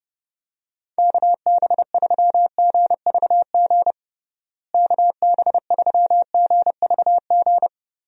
Morse code, heard as a keyed tone, about 32 wpm, 700 Hz. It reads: KB3GVG KB3GVG
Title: USA Call Signs - 6 Characters - Repeated 3x - 30wpm